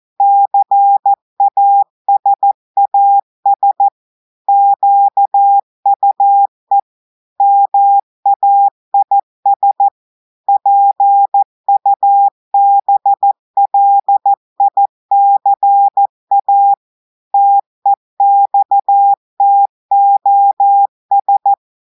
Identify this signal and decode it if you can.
Morse code, heard as a keyed tone, about 14 wpm, 800 Hz. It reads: CASAS QUE MAIS PUBLICA TEXTOS
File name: audios / portugais / portugais texte en clair 1 vitesse 14 mots minutes